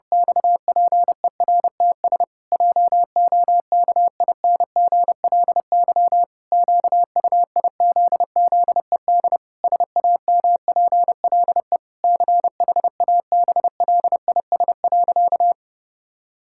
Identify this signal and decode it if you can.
Morse code, heard as a keyed tone, about 32 wpm, 700 Hz. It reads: XPERTS JOKINGLY QUIZZED SAMPLE CHABLIS.
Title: Pangram Challenge v2 - 30wpm